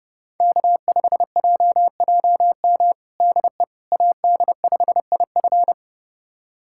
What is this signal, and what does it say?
Morse code, heard as a keyed tone, about 30 wpm, 700 Hz. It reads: K5JJM DE AD5IF
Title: USA Callsign DE Callsign - 5 Characters - 30wpm